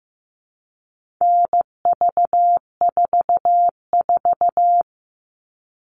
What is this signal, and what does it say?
Morse code, heard as a keyed tone, about 15 wpm, 700 Hz. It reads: NV44
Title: All Cut Numbers - 4-Digit 15wpm